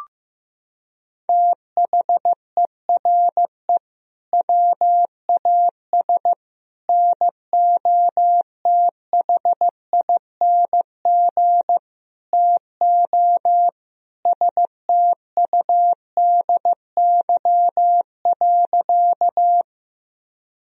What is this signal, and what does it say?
Morse code, heard as a keyed tone, about 15 wpm, 700 Hz. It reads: THERE WAS NOTHING TO STUDY.
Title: Sentences from Top 1000 Words - Review Part 1 - 15wpm